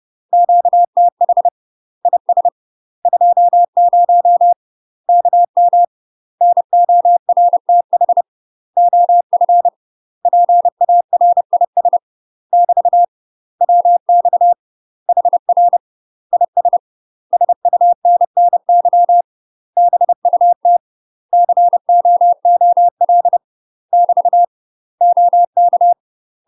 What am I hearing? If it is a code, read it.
Morse code, heard as a keyed tone, about 30 wpm, 700 Hz. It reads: QTH IS 20 KM NORTH OF PARIS <BT> WX HR IS SUNNY BUT COOL <BT> OK
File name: UK MASTER QSOs TRAINING / 30WPM / UK MASTER QSOs 0028